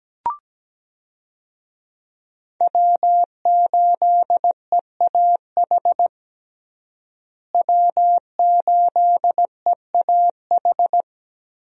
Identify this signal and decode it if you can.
Morse code, heard as a keyed tone, about 17 wpm, 700 Hz. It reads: W8EAH W8EAH
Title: USA Call Signs - 5 Characters - Repeated 2x - 17wpm